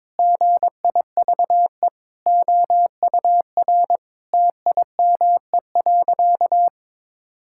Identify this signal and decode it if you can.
Morse code, heard as a keyed tone, about 22 wpm, 700 Hz. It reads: GIVE OUR TIME.
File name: practice / 22wpm / Sentences with 4-Word Phrases - 22wpm